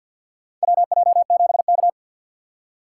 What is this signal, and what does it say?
Morse code, heard as a keyed tone, about 49 wpm, 700 Hz. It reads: WJ7K
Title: USA Call Signs - 3x Character Spacing - 50wpm